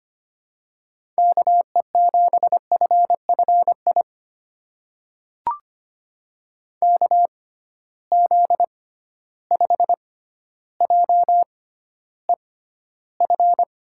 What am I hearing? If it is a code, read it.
Morse code, heard as a keyed tone, about 25 wpm, spaced as for 13 wpm, 700 Hz. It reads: KE7FFI KZ5JEF
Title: USA Call Signs - 6x Character Spacing - 25wpm